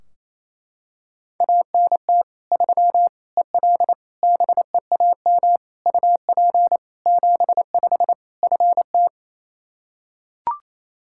Morse code, heard as a keyed tone, about 28 wpm, 700 Hz. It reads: ANT 3 EL BEAM UP 75 FT
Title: Practice QSO - Exchange 2 - Info - 28wpm